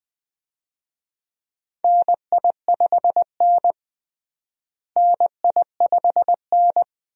Morse code, heard as a keyed tone, about 20 wpm, 700 Hz. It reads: NI5N NI5N
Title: USA Call Signs - 4 Characters - Repeated 2x - 20wpm